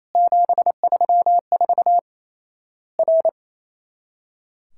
Morse code, heard as a keyed tone, about 28 wpm, 700 Hz. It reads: 734 R
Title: Worked All Europe DX Contest - QTC Exchange - 28wpm